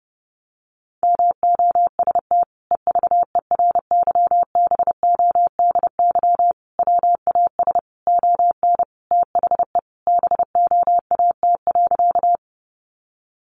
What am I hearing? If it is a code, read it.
Morse code, heard as a keyed tone, about 30 wpm, 700 Hz. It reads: MOST EVERYBODY WAS ON THE BOAT.
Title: Sentences from Top 1400 Words 30wpm